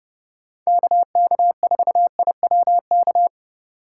Morse code, heard as a keyed tone, about 28 wpm, 700 Hz. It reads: KK4IWK